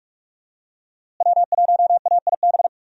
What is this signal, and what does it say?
Morse code, heard as a keyed tone, about 45 wpm, 700 Hz. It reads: W1AIB
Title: USA Call Signs - 5 Characters - Repeated 2x - 45wpm